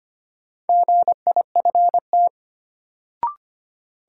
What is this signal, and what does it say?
Morse code, heard as a keyed tone, about 25 wpm, 700 Hz. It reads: GIFT
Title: Sentences from Top 900 Words - New Word Drill - 25wpm